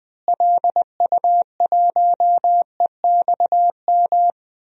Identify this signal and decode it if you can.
Morse code, heard as a keyed tone, about 20 wpm, 700 Hz. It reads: LU1EXM
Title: DXCC-100 Call Signs - 6 Characters - Repeated 3x - 20wpm